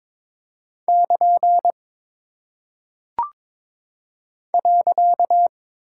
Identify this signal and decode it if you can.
Morse code, heard as a keyed tone, about 22 wpm, 700 Hz. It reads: <KN> .
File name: practice / 22wpm / Single Punctuation-Prosign - 22wpm